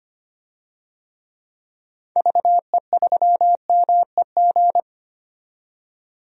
Morse code, heard as a keyed tone, about 25 wpm, 700 Hz. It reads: VE3MEG
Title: DXCC-100 Call Signs - 6 Characters - 25wpm